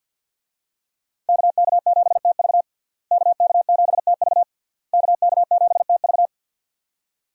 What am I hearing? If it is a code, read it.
Morse code, heard as a keyed tone, about 52 wpm, 700 Hz. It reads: KK7TV KK7TV KK7TV
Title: USA Call Signs - 5 Characters - Repeated 3x - 50wpm